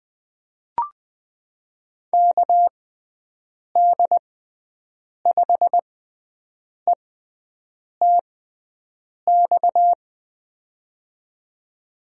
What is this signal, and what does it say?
Morse code, heard as a keyed tone, about 20 wpm, spaced as for 7 wpm, 700 Hz. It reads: KD5ETX
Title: USA Call Signs - 6 Characters - 6x Character Spacing - 20wpm